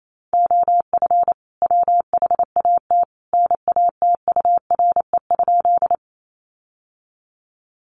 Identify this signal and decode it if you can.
Morse code, heard as a keyed tone, about 28 wpm, 700 Hz. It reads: OF WHAT NATURE?
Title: Sentences from Top 600 Words 28wpm